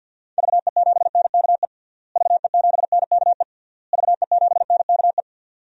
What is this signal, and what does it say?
Morse code, heard as a keyed tone, about 48 wpm, 700 Hz. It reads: VE7NKE VE7NKE VE7NKE
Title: DXCC-100 Call Signs - 6 Characters - Repeated 3x - 50wpm